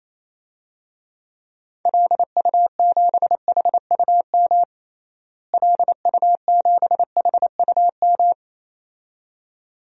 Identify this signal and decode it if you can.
Morse code, heard as a keyed tone, about 28 wpm, 700 Hz. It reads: LU7HUM LU7HUM
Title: DXCC-100 Call Signs - 6 Characters - Repeated 2x - 28wpm